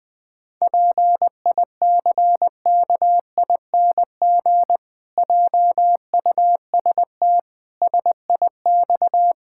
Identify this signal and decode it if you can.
Morse code, heard as a keyed tone, about 20 wpm, 700 Hz. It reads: PICKING JUST SIX
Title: Pangram Challenge v2 - 20wpm